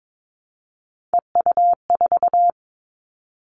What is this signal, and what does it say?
Morse code, heard as a keyed tone, about 22 wpm, 700 Hz. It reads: EU4
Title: Callsign Prefixes for Top 100 DXCC - 22wpm